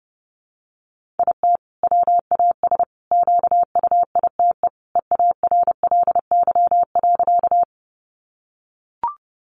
Morse code, heard as a keyed tone, about 30 wpm, 700 Hz. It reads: IT WAS QUITE EARLY.
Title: Sentences from Top 700 Words 30wpm